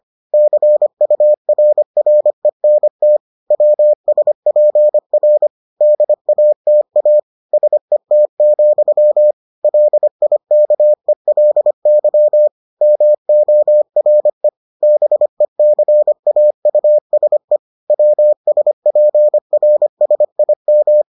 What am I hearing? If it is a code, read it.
Morse code, heard as a keyed tone, about 25 wpm, 600 Hz. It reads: CURRENT WSPR DATA SET, LIKELY MORE BECAUSE WSPRSIM